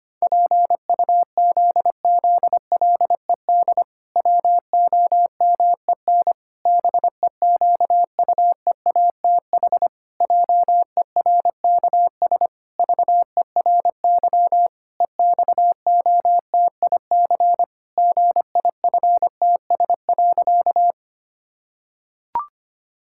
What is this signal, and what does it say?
Morse code, heard as a keyed tone, about 25 wpm, 700 Hz. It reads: PUZZLED WOMEN BEQUEATH JERKS VERY EXOTIC GIFTS.
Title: Pangram Challenge v2 - 25wpm